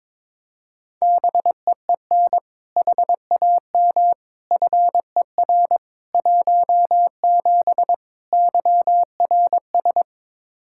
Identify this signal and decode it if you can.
Morse code, heard as a keyed tone, about 22 wpm, 700 Hz. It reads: BEEN HAM FER 17 YRS
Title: Practice QSO - Exchange 3 - More Info - 22wpm